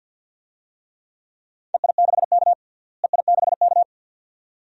Morse code, heard as a keyed tone, about 50 wpm, 700 Hz. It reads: EI6K EI6K
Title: DXCC-100 Call Signs - 4 Characters - Repeated 2x - 50wpm